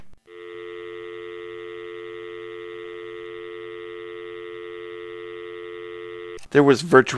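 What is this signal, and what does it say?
Dial tone (North America), 350 + 440 Hz, with speech.
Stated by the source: Rotary dial telephone.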